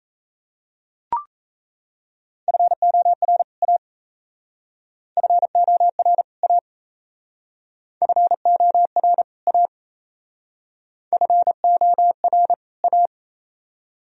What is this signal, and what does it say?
Morse code, heard as a keyed tone, about 37 wpm, 700 Hz. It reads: FOR A FOR A FOR A FOR A